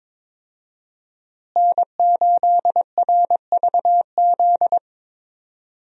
Morse code, heard as a keyed tone, about 22 wpm, 700 Hz. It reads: N8RVZ
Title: USA Call Signs - 5 Characters - Repeated 2x - 22wpm